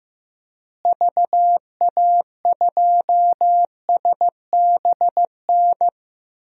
Morse code, heard as a keyed tone, about 15 wpm, 700 Hz. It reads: VA2SBN
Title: DXCC-100 Call Signs - 6 Characters - Repeated 2x - 15wpm